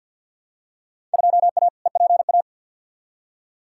Morse code, heard as a keyed tone, about 47 wpm, 700 Hz. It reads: 2A EPA
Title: Field Day Exchanges - 50wpm